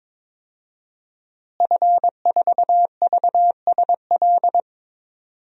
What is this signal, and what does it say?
Morse code, heard as a keyed tone, about 22 wpm, 700 Hz. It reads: F4VSL
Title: DXCC-100 Call Signs - 5 Characters - Repeated 2x - 22wpm